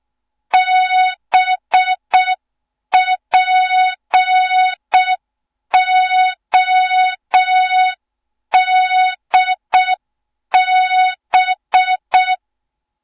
Morse code, heard as a keyed tone, about 6 wpm, 3700 Hz. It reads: BPODB
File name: Morze-HA2MN / hang / 07 GYAKORLAT